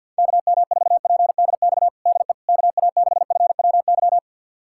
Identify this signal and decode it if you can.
Morse code, heard as a keyed tone, about 50 wpm, 700 Hz. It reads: KM4PDX DE KA6FWY
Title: USA Callsign DE Callsign - 6 Characters - 50wpm